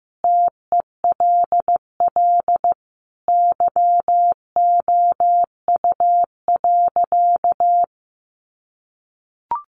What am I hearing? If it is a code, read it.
Morse code, heard as a keyed tone, about 14 wpm, 700 Hz. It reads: TELL YOU.
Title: Sentences from Top 1100 Words 15wpm